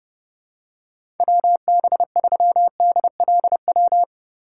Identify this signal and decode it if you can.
Morse code, heard as a keyed tone, about 30 wpm, 700 Hz. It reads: WB3DLW
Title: USA Call Signs - 6 Characters - 6x Character Spacing - 30wpm